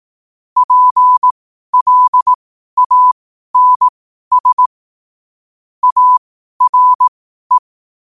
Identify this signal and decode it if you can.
Morse code, heard as a keyed tone, about 18 wpm, spaced as for 12 wpm, 1000 Hz. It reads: PLANS ARE
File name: qotdfiles / 12wpm